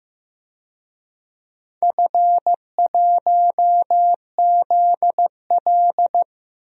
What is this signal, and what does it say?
Morse code, heard as a keyed tone, about 16 wpm, 700 Hz. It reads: F1ZL
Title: DXCC-100 Call Signs - 4 Characters - Repeated 2x - 15wpm